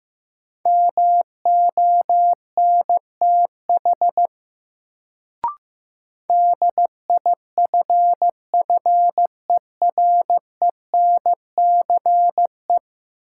Morse code, heard as a keyed tone, about 15 wpm, 700 Hz. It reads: MONTH DIFFERENCE
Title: Sentences from Top 800 Words - New Word Drill - 15wpm